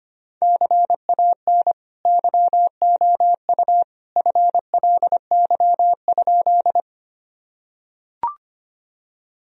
Morse code, heard as a keyed tone, about 25 wpm, 700 Hz. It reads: CAN YOU FLY?
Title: Sentences from Top 1000 Words - Review Part 2 - 25wpm